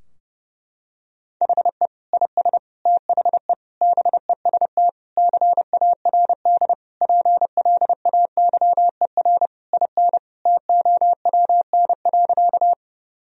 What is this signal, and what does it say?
Morse code, heard as a keyed tone, about 30 wpm, 700 Hz. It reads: HE IS THE BEST CARD PLAYER IN TOWN.
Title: Sentences from Top 1000 Words - Review Part 4 - 30wpm